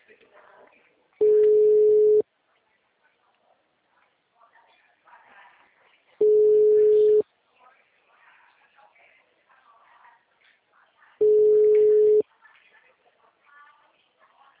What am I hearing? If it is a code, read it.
Ringback tone (Europe, 425 Hz), with speech.
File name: out- -2012- - - .4